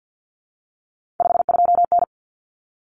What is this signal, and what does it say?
Morse code, heard as a keyed tone, about 50 wpm, 700 Hz. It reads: 52N
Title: Common Cut Numbers - 3-Digit 50wpm